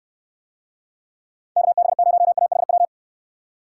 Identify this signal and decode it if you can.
Morse code, heard as a keyed tone, about 69 wpm, 700 Hz. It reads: KB0AHW